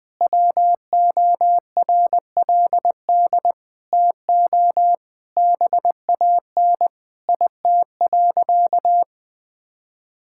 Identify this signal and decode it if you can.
Morse code, heard as a keyed tone, about 21 wpm, 700 Hz. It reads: WORLD TO BAN IT.
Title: Sentences with 8-Word Phrases - 20wpm